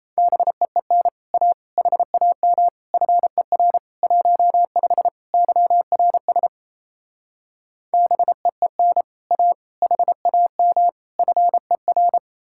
Morse code, heard as a keyed tone, about 37 wpm, 700 Hz. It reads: BEEN A HAM FER 15 YRS BEEN A HAM FER